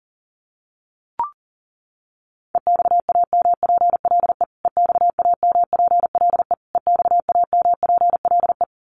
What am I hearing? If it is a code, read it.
Morse code, heard as a keyed tone, about 40 wpm, 700 Hz. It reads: EXAMPLE EXAMPLE EXAMPLE
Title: Top 500 Words Repeated 40wpm